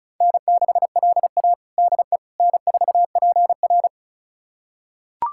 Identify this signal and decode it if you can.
Morse code, heard as a keyed tone, about 36 wpm, 700 Hz. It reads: N6LA DE N4PR
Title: USA Callsign DE Callsign - 4 Characters - 35wpm